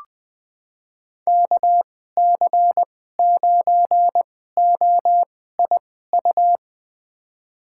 Morse code, heard as a keyed tone, about 20 wpm, spaced as for 14 wpm, 700 Hz. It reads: KC9OIU
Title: USA Call Signs - 6 Characters - 2x Character Spacing - 20wpm